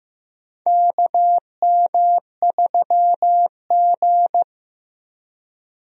Morse code, heard as a keyed tone, about 15 wpm, 700 Hz. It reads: KM3G
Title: USA Call Signs - 4 Characters - Repeated 2x - 15wpm